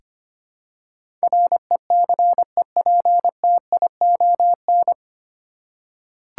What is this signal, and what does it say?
Morse code, heard as a keyed tone, about 25 wpm, 700 Hz. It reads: RECEPTION
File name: practice / 25wpm / 9 Letter Words - First 3 Letters Given - 25wpm